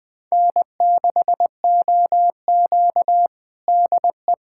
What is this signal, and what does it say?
Morse code, heard as a keyed tone, about 20 wpm, 700 Hz. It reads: N6OQ DE
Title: USA Callsign DE Callsign - 4 Characters - 20wpm